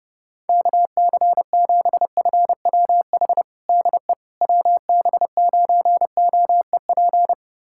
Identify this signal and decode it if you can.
Morse code, heard as a keyed tone, about 30 wpm, 700 Hz. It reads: KC7FWH DE WB9OEP
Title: USA Callsign DE Callsign - 6 Characters - 30wpm